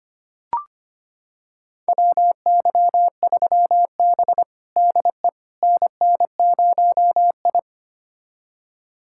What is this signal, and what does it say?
Morse code, heard as a keyed tone, about 25 wpm, 700 Hz. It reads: WY3B DE NN0I
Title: USA Callsign DE Callsign - 4 Characters - 25wpm